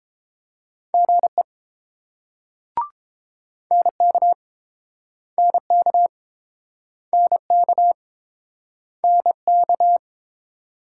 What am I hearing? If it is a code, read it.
Morse code, heard as a keyed tone, about 30 wpm, 700 Hz. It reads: GE NK NK NK NK